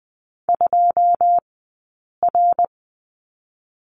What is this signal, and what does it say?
Morse code, heard as a keyed tone, about 20 wpm, spaced as for 8 wpm, 700 Hz. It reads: 2R
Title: 080 Lesson 14d - Number 2 - Two Characters 20wpm